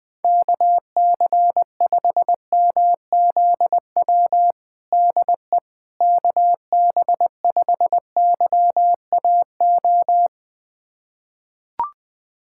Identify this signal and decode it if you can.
Morse code, heard as a keyed tone, about 20 wpm, 700 Hz. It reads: KC5MZW DE KB5YAO
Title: USA Callsign DE Callsign - 6 Characters - 20wpm